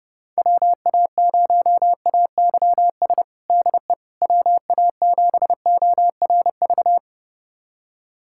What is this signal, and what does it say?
Morse code, heard as a keyed tone, about 30 wpm, 690 Hz. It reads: WA0AYS DE WA7ORV
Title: USA Callsign DE Callsign - 6 Characters - 30wpm